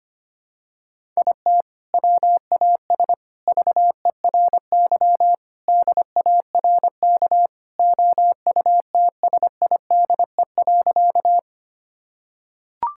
Morse code, heard as a keyed tone, about 25 wpm, 700 Hz. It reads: IT WAS VERY DARK OUTSIDE.
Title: Sentences from Top 1000 Words - Review Part 6 - 25wpm